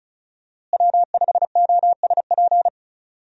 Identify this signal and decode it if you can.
Morse code, heard as a keyed tone, about 36 wpm, 700 Hz. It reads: W5OSP
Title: USA Call Signs - 5 Characters - Repeated 3x - 35wpm